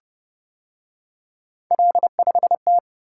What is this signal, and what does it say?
Morse code, heard as a keyed tone, about 29 wpm, 700 Hz. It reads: L5T